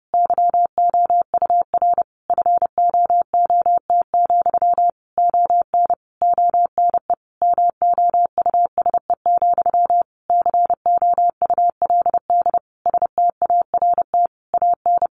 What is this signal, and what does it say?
Morse code, heard as a keyed tone, about 30 wpm, 700 Hz. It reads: YOUR FOOT, ON ONE MOUSE, COULD START AN
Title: A Sound Of Thunder 30wpm